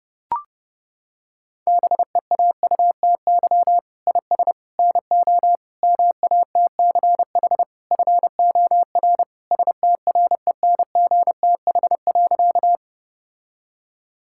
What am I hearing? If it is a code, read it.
Morse code, heard as a keyed tone, about 30 wpm, 700 Hz. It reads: BEAUTY IS NO MATCH FOR STRENGTH.
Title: Sentences from Top 1600 Words 30wpm